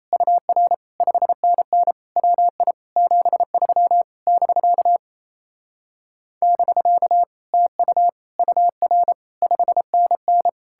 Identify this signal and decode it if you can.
Morse code, heard as a keyed tone, about 38 wpm, 700 Hz. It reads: UR 5NN WI 73 <BK> <BK> TU UR 5NN